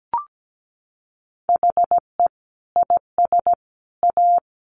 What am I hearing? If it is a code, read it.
Morse code, heard as a keyed tone, about 17 wpm, 700 Hz. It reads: HE IS A
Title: Sentences from Top 1100 Words 17wpm